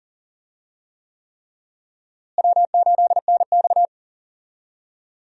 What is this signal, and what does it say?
Morse code, heard as a keyed tone, about 40 wpm, 700 Hz. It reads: W8NX
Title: USA Call Signs - 4 Characters - Repeated 3x - 40wpm